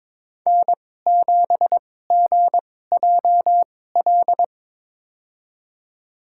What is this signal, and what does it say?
Morse code, heard as a keyed tone, about 22 wpm, spaced as for 16 wpm, 700 Hz. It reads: N7GJL
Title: USA Call Signs - 5 Characters - 2x Character Spacing - 22wpm